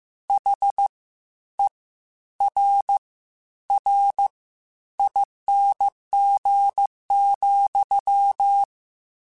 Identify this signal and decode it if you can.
Morse code, heard as a keyed tone, about 15 wpm, spaced as for 10 wpm, 800 Hz. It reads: HERRING,
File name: Misc / Misc-mp3s / WordPairs sp @13